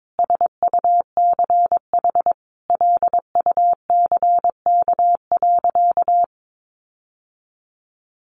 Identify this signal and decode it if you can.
Morse code, heard as a keyed tone, about 22 wpm, 700 Hz. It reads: SUCH LUCK.
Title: Sentences from Top 1500 Words 22wpm